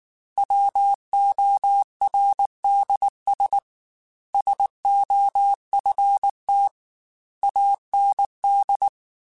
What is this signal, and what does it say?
Morse code, heard as a keyed tone, about 19 wpm, 800 Hz. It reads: WORDS SOFT AND
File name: Chuckles / Chuckles-mp3s / SoftWords @18